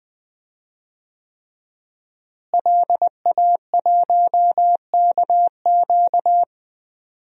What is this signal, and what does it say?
Morse code, heard as a keyed tone, about 20 wpm, 700 Hz. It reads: LA1KQ
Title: DXCC-100 Call Signs - 5 Characters - Repeated 3x - 20wpm